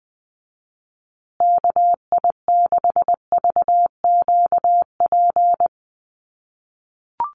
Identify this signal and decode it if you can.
Morse code, heard as a keyed tone, about 20 wpm, 700 Hz. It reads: KI6VQP
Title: USA Call Signs Repeated 3x 20wpm